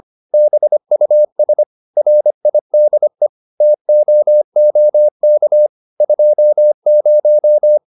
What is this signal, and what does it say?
Morse code, heard as a keyed tone, about 25 wpm, 600 Hz. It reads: BUS RIDE TOOK 20